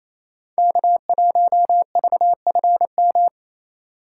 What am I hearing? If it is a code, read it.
Morse code, heard as a keyed tone, about 34 wpm, 700 Hz. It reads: K1VFM
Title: Speed Racing - USA 1x3 Call Signs 28wpm